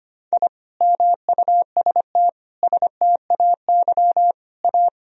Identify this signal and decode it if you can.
Morse code, heard as a keyed tone, about 25 wpm, 700 Hz. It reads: I MUST STAY A
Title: Sentences from Top 1000 Words - Review Part 1 - 25wpm